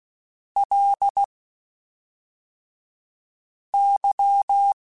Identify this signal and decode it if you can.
Morse code, heard as a keyed tone, about 16 wpm, 800 Hz. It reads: L Y
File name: Basic Sets / Basic Letter Groups Collective7 / LFCYQ@5